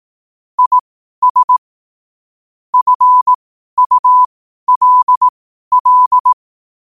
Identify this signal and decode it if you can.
Morse code, heard as a keyed tone, about 18 wpm, spaced as for 12 wpm, 1000 Hz. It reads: IS FULL